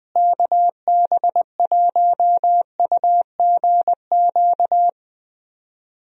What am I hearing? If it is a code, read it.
Morse code, heard as a keyed tone, about 21 wpm, 700 Hz. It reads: KB1UGQ
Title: USA Call Signs - 6 Characters - Repeated 3x - 20wpm